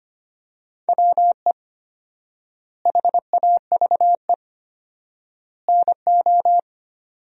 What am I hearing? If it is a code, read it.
Morse code, heard as a keyed tone, about 25 wpm, 700 Hz. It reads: WE HAVE NO
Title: 4x Word-Spacing - Sentences from Top 100 Words - 25wpm